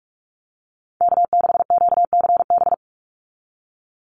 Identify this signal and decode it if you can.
Morse code, heard as a keyed tone, about 44 wpm, 700 Hz. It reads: K6QCB
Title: NA SOTA Chaser Callsigns 45wpm